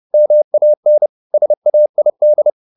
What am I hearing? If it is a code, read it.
Morse code, heard as a keyed tone, about 30 wpm, 600 Hz. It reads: MAN SAID